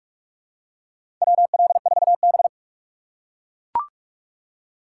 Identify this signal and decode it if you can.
Morse code, heard as a keyed tone, about 42 wpm, 700 Hz. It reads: WL4B
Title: USA Call Signs - 4 Characters - 2x Character Spacing - 45wpm